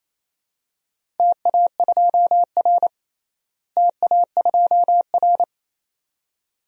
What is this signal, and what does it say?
Morse code, heard as a keyed tone, about 28 wpm, 700 Hz. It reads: TA2R TA2R
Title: DXCC-100 Call Signs - 4 Characters - Repeated 2x - 28wpm